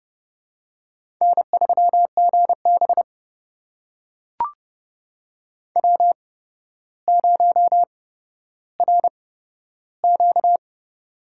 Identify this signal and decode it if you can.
Morse code, heard as a keyed tone, about 30 wpm, spaced as for 13 wpm, 700 Hz. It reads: N3GB W0RQ
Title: USA Call Signs - 4 Characters - 8x Character Spacing - 30wpm